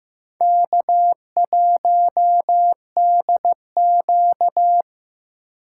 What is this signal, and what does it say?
Morse code, heard as a keyed tone, about 15 wpm, 700 Hz. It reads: K1DQ
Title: USA Call Signs - 4 Characters - 6x Character Spacing - 15wpm